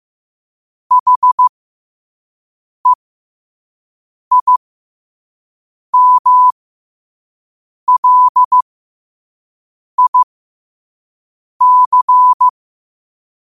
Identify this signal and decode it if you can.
Morse code, heard as a keyed tone, about 15 wpm, spaced as for 5 wpm, 1000 Hz. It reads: HEIMLIC
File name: qotdfiles / 5wpm